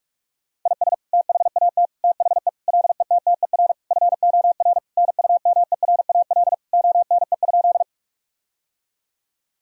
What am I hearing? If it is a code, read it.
Morse code, heard as a keyed tone, about 41 wpm, 700 Hz. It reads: IS THAT THE LETTER FOR NUMERAL ONE?
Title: Sentences from Top 1000 Words - Review Part 2 - 45wpm